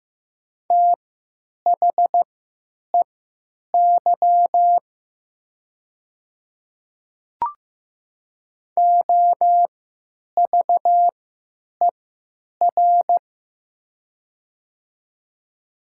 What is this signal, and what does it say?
Morse code, heard as a keyed tone, about 15 wpm, spaced as for 9 wpm, 700 Hz. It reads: THEY OVER
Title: 3x Character Spacing - Top 100 Words - 15wpm